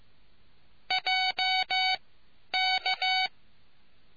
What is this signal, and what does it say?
Morse code, heard as a keyed tone, about 15 wpm, spaced as for 10 wpm, 750 Hz. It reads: JK